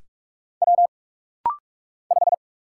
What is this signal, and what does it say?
Morse code, heard as a keyed tone, about 44 wpm, 700 Hz. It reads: W 5
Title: Single Letter-Number Alt - Rapid-Fire 45wpm